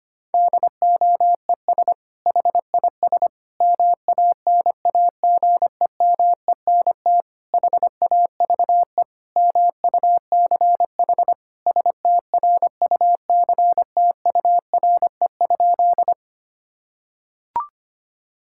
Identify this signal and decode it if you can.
Morse code, heard as a keyed tone, about 25 wpm, 700 Hz. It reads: DOES HIS MANAGEMENT HAVE MUCH STRUCTURE?
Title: Sentences from Top 1000 Words - Review Part 6 - 25wpm